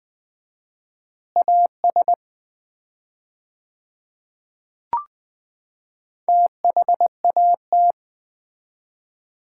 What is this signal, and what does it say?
Morse code, heard as a keyed tone, about 20 wpm, 700 Hz. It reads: AS THAT
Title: 4x Word-Spacing - Sets of 3 Words - 20wpm